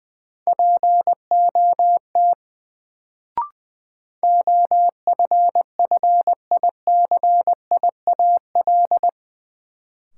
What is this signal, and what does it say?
Morse code, heard as a keyed tone, about 20 wpm, 700 Hz. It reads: POT OFFICIAL
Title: Sentences from Top 1000 Words - New Word Drill - 20wpm